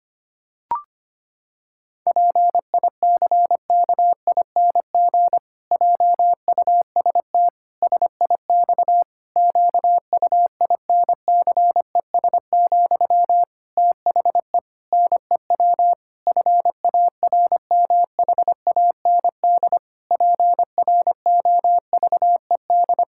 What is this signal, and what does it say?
Morse code, heard as a keyed tone, about 25 wpm, 700 Hz. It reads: PICKING JUST SIX QUINCES, THE NEW FARMHAND PROVED
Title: Pangram Challenge v2 - 25wpm